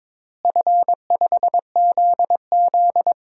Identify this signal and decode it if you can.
Morse code, heard as a keyed tone, about 22 wpm, 700 Hz. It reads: F5ZZ
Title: DXCC-100 Call Signs - 4 Characters - Repeated 3x - 22wpm